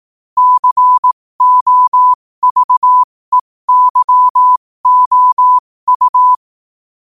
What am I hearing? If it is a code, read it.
Morse code, heard as a keyed tone, about 18 wpm, spaced as for 15 wpm, 1000 Hz. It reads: COVEYOU
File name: qotdfiles / 15wpm